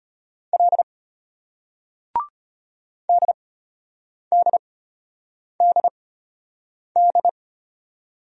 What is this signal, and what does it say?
Morse code, heard as a keyed tone, about 33 wpm, spaced as for 8 wpm, 700 Hz. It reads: L DDDD